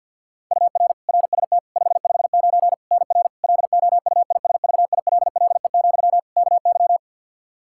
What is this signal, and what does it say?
Morse code, heard as a keyed tone, about 53 wpm, 700 Hz. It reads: UR RST 559 NR LOUISVILLE, KY
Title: K1USN SST QSO Practice 50wpm